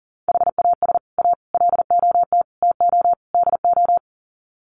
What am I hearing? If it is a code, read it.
Morse code, heard as a keyed tone, about 40 wpm, 700 Hz. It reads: HAS A LOT TO DO
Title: Sets of 5 Words Encore2 40wpm